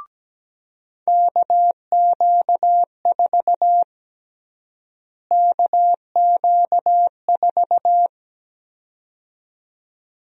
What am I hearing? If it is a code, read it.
Morse code, heard as a keyed tone, about 17 wpm, 700 Hz. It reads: KQ4 KQ4
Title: USA Call Sign Prefixes - Repeated 2x - 17wpm